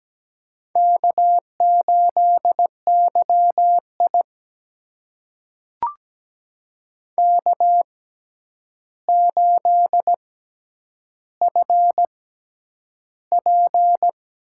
Morse code, heard as a keyed tone, about 17 wpm, spaced as for 9 wpm, 700 Hz. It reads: K8YI K8FP
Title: USA Call Signs - 4 Characters - 6x Character Spacing - 17wpm